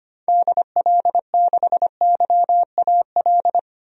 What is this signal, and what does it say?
Morse code, heard as a keyed tone, about 25 wpm, 700 Hz. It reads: DL6YAL
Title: DXCC-100 Call Signs - 6 Characters - Repeated 3x - 25wpm